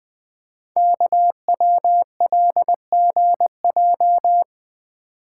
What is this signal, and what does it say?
Morse code, heard as a keyed tone, about 21 wpm, 700 Hz. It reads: KWLGJ
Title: Code Groups - 5 Letters - 20wpm